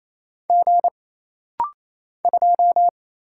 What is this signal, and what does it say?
Morse code, heard as a keyed tone, about 28 wpm, 700 Hz. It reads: G 2
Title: Single Letter-Number - Rapid-Fire 28wpm